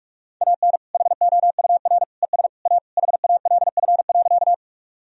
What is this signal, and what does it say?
Morse code, heard as a keyed tone, about 46 wpm, 700 Hz. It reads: AN HOUR ES A HALF.
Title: Sentences from Top 400 Words and Abbreviations 45wpm